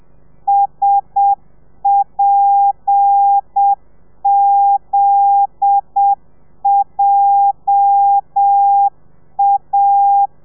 Morse code, heard as a keyed tone, about 7 wpm, 800 Hz. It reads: SPZJA